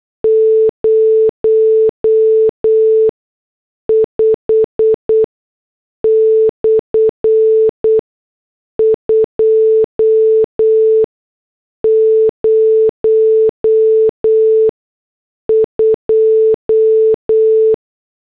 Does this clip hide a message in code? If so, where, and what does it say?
Yes: Morse code from 0.2 to 17.7 s, heard as a keyed tone, about 8 wpm, spaced as for 6 wpm, 440 Hz. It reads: 05/202